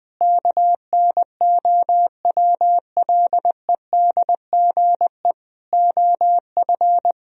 Morse code, heard as a keyed tone, about 21 wpm, 700 Hz. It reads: KNOWLEDGE OF
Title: Sentences from Top 1000 Words - Review Part 3 - 20wpm